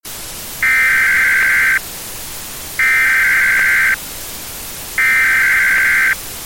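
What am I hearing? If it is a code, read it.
Emergency Alert System (SAME) tones (header bursts), heard as harsh buzzing bursts. It is heard from 0.5 to 6.2 s.
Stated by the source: Emergency alert tones used in a game map.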